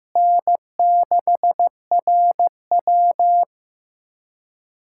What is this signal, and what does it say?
Morse code, heard as a keyed tone, about 15 wpm, 700 Hz. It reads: N6RW
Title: USA Callsign DE Callsign - 4 Characters - 15wpm